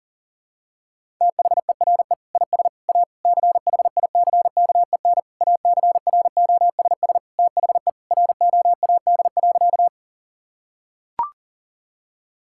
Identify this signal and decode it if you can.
Morse code, heard as a keyed tone, about 40 wpm, 700 Hz. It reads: THERE IS A CHICKEN ACROSS THE ROAD.
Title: Sentences from Top 1000 Words - Review Part 5 - 40wpm